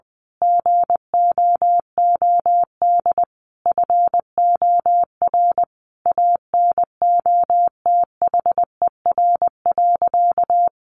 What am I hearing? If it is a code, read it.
Morse code, heard as a keyed tone, about 19 wpm, 700 Hz. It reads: GOOD FOR ANOTHER.
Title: Sentences from Top 1400 Words 20wpm